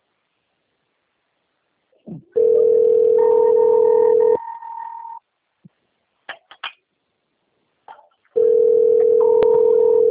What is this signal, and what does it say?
Ringback tone (North America), 440 + 480 Hz.